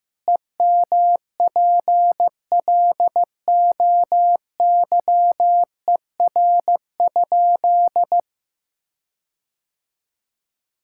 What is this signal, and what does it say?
Morse code, heard as a keyed tone, about 15 wpm, 700 Hz. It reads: EMPLOYER?
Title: Sentences from Top 1600 Words 15wpm